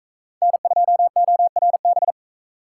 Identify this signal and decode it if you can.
Morse code, heard as a keyed tone, about 44 wpm, 700 Hz. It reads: N2ORB